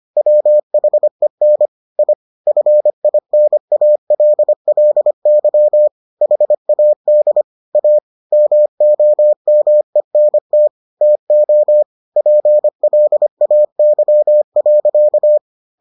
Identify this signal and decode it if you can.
Morse code, heard as a keyed tone, about 25 wpm, 600 Hz. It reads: WHEN I FINALLY HAD A MOMENT TO PLAY.